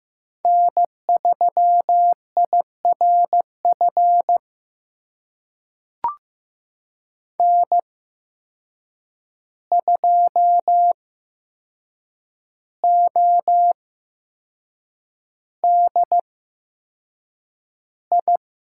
Morse code, heard as a keyed tone, about 15 wpm, spaced as for 6 wpm, 700 Hz. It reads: N3IRF N2ODI
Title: USA Call Signs - 5 Characters - 8x Character Spacing - 15wpm